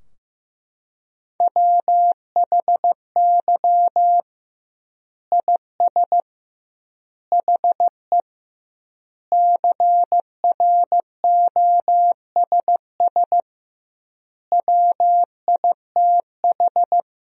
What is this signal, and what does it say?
Morse code, heard as a keyed tone, about 16 wpm, 700 Hz. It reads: WHY IS HE CROSS WITH
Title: 2x Word-Spacing - Sentences from Top 300 Words - 15wpm